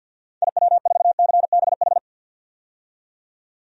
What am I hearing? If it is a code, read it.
Morse code, heard as a keyed tone, about 46 wpm, 700 Hz. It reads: IW4CBH
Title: DXCC-100 Call Signs - 6 Characters - 50wpm